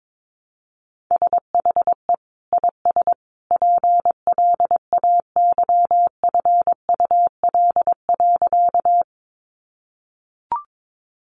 Morse code, heard as a keyed tone, about 22 wpm, 700 Hz. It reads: SHE IS PLAYFUL.
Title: Easy Phrases 22wpm